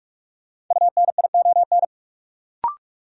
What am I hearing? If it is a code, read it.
Morse code, heard as a keyed tone, about 46 wpm, 700 Hz. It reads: UNION
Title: Sets of 5 Words Encore - New Word Drill - 45wpm